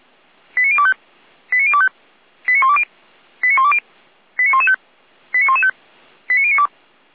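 Five-tone selective call (ZVEI), heard as quick runs of five tones, tones of about 70 ms. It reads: 89025 89025 89120 89120 89115 89115 89092
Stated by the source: ZVEI five-tone selective call 89280, received by SDR.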